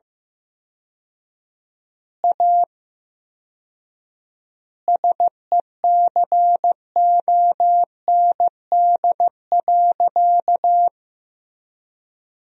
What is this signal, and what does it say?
Morse code, heard as a keyed tone, about 15 wpm, 700 Hz. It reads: A SECOND.
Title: 4x Word-Spacing - Sentences from Top 500 Words - 15wpm